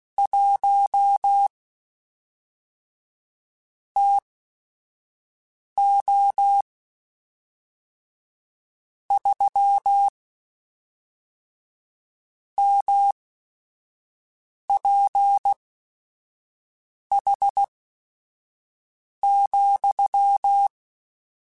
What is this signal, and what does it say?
Morse code, heard as a keyed tone, about 16 wpm, spaced as for 5 wpm, 800 Hz. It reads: 1 TO 3 MPH,
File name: Weather Sets / Weather Sets-mp3s / Weather 5 1 @5